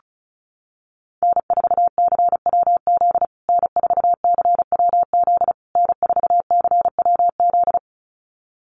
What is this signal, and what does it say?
Morse code, heard as a keyed tone, about 34 wpm, 700 Hz. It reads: N4CWZ N4CWZ N4CWZ
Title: USA Call Signs Repeated 3x 35wpm